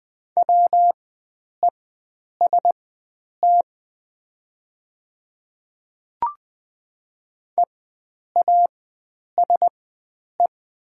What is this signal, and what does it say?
Morse code, heard as a keyed tone, about 20 wpm, spaced as for 9 wpm, 700 Hz. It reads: WEST EASE
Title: 4x Character Spacing - Top 500 Words - 20wpm